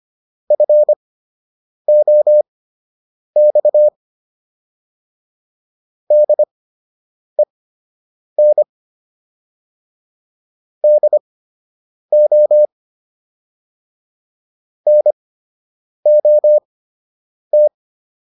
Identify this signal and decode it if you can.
Morse code, heard as a keyed tone, about 25 wpm, spaced as for 8 wpm, 600 Hz. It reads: FOX DEN DO NOT